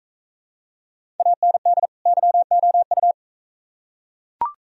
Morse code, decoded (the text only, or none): AND YOU